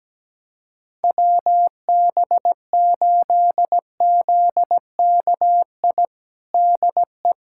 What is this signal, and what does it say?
Morse code, heard as a keyed tone, about 17 wpm, 700 Hz. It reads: WB8ZKI DE
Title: USA Callsign DE Callsign - 6 Characters - 17wpm